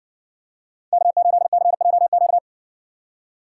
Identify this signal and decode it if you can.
Morse code, heard as a keyed tone, about 60 wpm, 700 Hz. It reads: K8CJQ